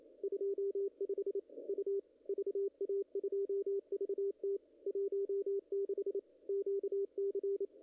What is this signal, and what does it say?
Morse code, heard as a keyed tone, about 28 wpm, 410 Hz, with radio noise. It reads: 25 U VA2VT 16 QC
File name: contests / ss-cw-k5tr-op-n5rz -0100